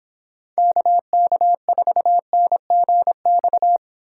Morse code, heard as a keyed tone, about 26 wpm, 700 Hz. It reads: KK4NGX